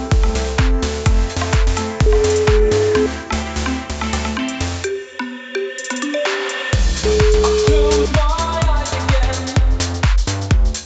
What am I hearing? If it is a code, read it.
Ringback tone (Europe, 425 Hz), with noise.